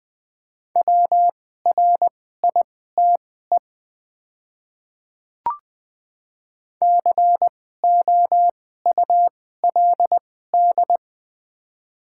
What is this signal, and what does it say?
Morse code, heard as a keyed tone, about 20 wpm, spaced as for 14 wpm, 700 Hz. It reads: WRITE COULD
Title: 2x Character Spacing - Top 500 Words - 20wpm